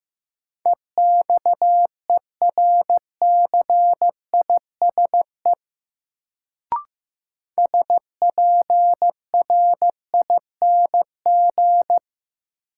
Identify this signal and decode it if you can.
Morse code, heard as a keyed tone, about 15 wpm, 700 Hz. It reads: EXERCISE SPRING
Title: Sentences from Top 1200 Words - New Word Drill Alt - 15wpm